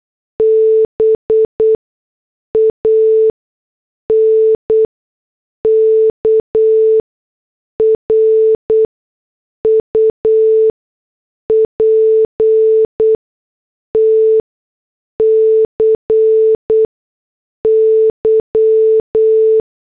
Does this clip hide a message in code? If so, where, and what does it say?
Yes: Morse code from 0.4 to 19.6 s, heard as a keyed tone, about 8 wpm, spaced as for 6 wpm, 440 Hz. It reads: BANKRUPTCY